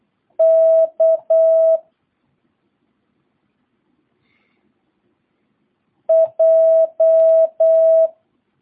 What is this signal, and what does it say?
Morse code, heard as a keyed tone, about 8 wpm, 650 Hz. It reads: K J